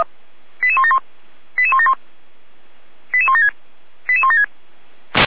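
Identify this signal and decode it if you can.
Five-tone selective call (ZVEI), heard as quick runs of five tones, tones of about 70 ms. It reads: 88171 88171 88176 88176
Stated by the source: Five-tone ZVEI selective call received on SDR.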